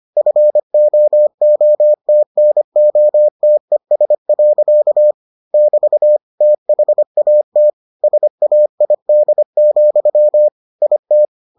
Morse code, heard as a keyed tone, about 25 wpm, 600 Hz. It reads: FOOTNOTES. <BT> THAT SAID, IT